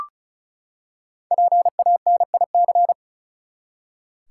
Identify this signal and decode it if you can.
Morse code, heard as a keyed tone, about 36 wpm, 700 Hz. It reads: PANIC
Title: Five Letter Words - 35wpm